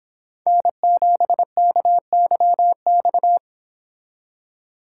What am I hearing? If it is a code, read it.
Morse code, heard as a keyed tone, about 26 wpm, 700 Hz. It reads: N7KYX
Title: Speed Racing - USA Call Signs 22wpm